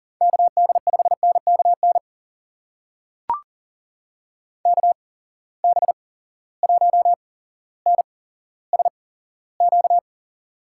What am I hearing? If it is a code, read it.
Morse code, heard as a keyed tone, about 40 wpm, spaced as for 17 wpm, 700 Hz. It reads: KD5NKN KB1NSQ
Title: USA Call Signs - 6 Characters - 8x Character Spacing - 40wpm